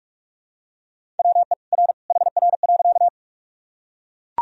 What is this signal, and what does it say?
Morse code, heard as a keyed tone, about 42 wpm, 700 Hz. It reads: WE R HR.
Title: Sentences from Top 200 Words and Abbreviations 45wpm